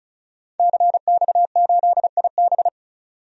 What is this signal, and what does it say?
Morse code, heard as a keyed tone, about 36 wpm, 700 Hz. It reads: CX8IB
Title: DXCC-100 Call Signs - 5 Characters - 35wpm